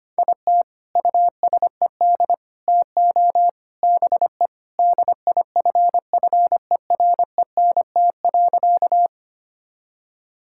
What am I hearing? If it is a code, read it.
Morse code, heard as a keyed tone, about 25 wpm, 700 Hz. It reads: IT USED TO BE DIFFERENT.
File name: practice / 25wpm / Sentences with 4-Word Phrases - 25wpm